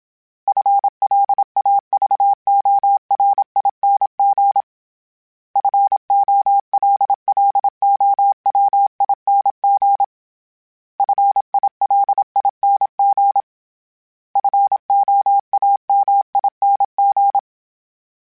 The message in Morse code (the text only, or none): FLAVORING FOLLOWING FILING FOAMING